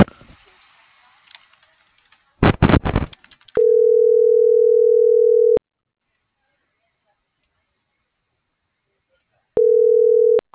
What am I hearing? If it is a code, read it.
Ringback tone (North America), 440 + 480 Hz.